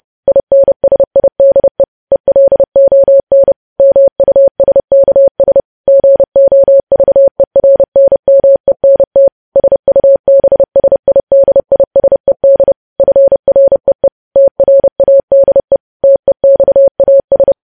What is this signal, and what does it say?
Morse code, heard as a keyed tone, about 30 wpm, 550 Hz. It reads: INSIDE ELON MUSKS GOVERNMENT SUBSIDISED FREE TRADE TEXAS
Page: news headlines in morse code at 30 wpm